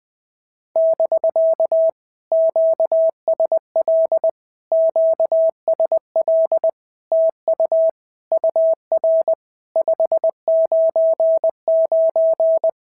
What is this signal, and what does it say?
Morse code, heard as a keyed tone, about 20 wpm, 650 Hz. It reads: <BK> QSL QSL TU UR 599
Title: POTA QSO Practice - 20wpm